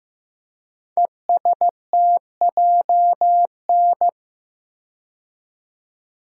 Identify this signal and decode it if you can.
Morse code, heard as a keyed tone, about 15 wpm, 700 Hz. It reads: ESTJN
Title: Code Groups - 5 Letter:Numbers - 15wpm